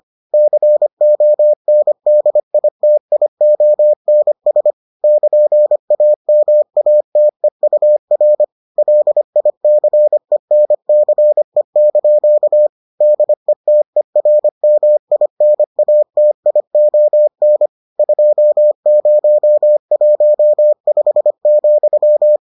Morse code, heard as a keyed tone, about 25 wpm, 600 Hz. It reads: CONDITIONS <KN>AMATEUR LICENCE) DETERMINATION 2015,